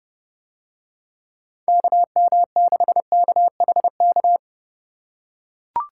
Morse code, heard as a keyed tone, about 31 wpm, 700 Hz. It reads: KM6KHK